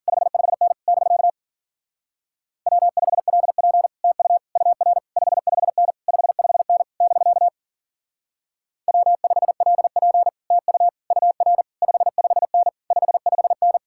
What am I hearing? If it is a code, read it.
Morse code, heard as a keyed tone, about 50 wpm, 700 Hz. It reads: 55N <BK> W5LP TU UR 55N 55N <BK> W5LP TU UR 55N 55N
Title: Speed-Racing Common QSO Phrases - 40wpm